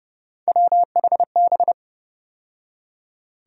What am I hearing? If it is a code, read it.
Morse code, heard as a keyed tone, about 30 wpm, 700 Hz. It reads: WHB
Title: USA Call Sign Postfixes - 3 Letters 30wpm